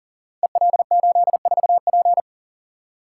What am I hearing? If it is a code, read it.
Morse code, heard as a keyed tone, about 39 wpm, 690 Hz. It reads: EL84P
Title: Code Groups - 5 Letter:Numbers - 40wpm